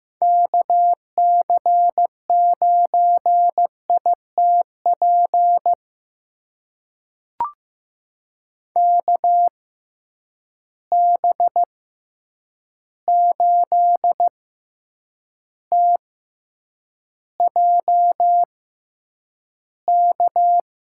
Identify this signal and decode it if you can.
Morse code, heard as a keyed tone, about 15 wpm, spaced as for 8 wpm, 700 Hz. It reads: KC9ITP KB8TJK
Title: USA Call Signs - 6 Characters - 6x Character Spacing - 15wpm